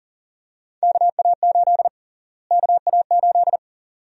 Morse code, heard as a keyed tone, about 40 wpm, 700 Hz. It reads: KA8 KA8
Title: USA Call Sign Prefixes - Repeated 2x - 40wpm